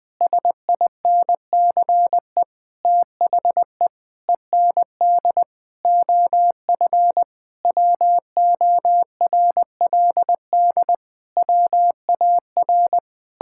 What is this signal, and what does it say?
Morse code, heard as a keyed tone, about 20 wpm, 700 Hz. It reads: SINCE THE END OF WORLD WAR
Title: Sentences with 7-Word Phrases - Spoken Beginning - 20wpm